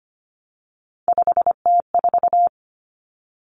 Morse code, heard as a keyed tone, about 25 wpm, 700 Hz. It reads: 5T4
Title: Common Cut Numbers - 3-Digit 25wpm